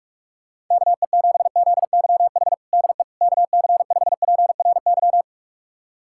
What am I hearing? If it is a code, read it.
Morse code, heard as a keyed tone, about 44 wpm, 700 Hz. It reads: KE7ZYH DE KC5PRY
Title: USA Callsign DE Callsign - 6 Characters - 45wpm